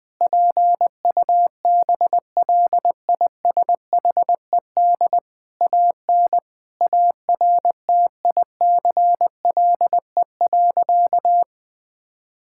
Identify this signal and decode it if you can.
Morse code, heard as a keyed tone, about 20 wpm, 700 Hz. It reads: PUBLISHED AN ARTICLE.
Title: Sentences with 4-Word Phrases - Spoken Beginning - 20wpm